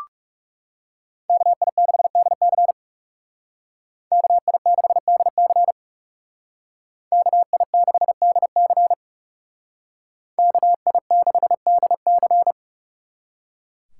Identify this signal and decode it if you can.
Morse code, heard as a keyed tone, about 36 wpm, 700 Hz. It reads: KI6DC KI6DC KI6DC KI6DC
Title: Speed Racing - USA 2x2 Call Signs 30wpm